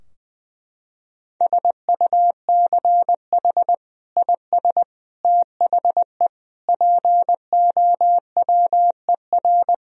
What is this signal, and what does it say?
Morse code, heard as a keyed tone, about 20 wpm, 700 Hz. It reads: SUCH IS THE POWER
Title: Sentences from Top 1000 Words - Review Part 2 - 20wpm